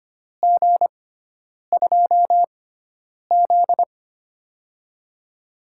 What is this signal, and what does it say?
Morse code, heard as a keyed tone, about 25 wpm, spaced as for 9 wpm, 700 Hz. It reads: G2Z
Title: USA Call Signs - 4 Characters - 6x Character Spacing - 25wpm